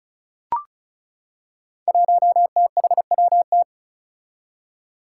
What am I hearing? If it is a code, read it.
Morse code, heard as a keyed tone, about 34 wpm, 700 Hz. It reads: 1THWT
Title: Five Challenging Letter-Number Combinations 35wpm